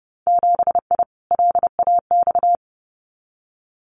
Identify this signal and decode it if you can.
Morse code, heard as a keyed tone, about 30 wpm, 700 Hz. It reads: 7I LAX
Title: Winter Field Day Exchanges Encore 30wpm